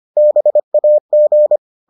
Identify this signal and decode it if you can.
Morse code, heard as a keyed tone, about 25 wpm, 600 Hz. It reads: BAG